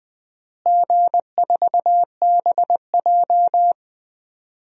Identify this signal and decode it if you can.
Morse code, heard as a keyed tone, about 20 wpm, 700 Hz. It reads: G4BJ
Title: DXCC-100 Call Signs - 4 Characters - Repeated 2x - 20wpm